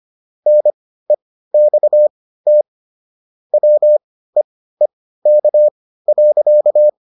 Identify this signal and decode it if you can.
Morse code, heard as a keyed tone, about 25 wpm, spaced as for 15 wpm, 600 Hz. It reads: NEXT WEEK.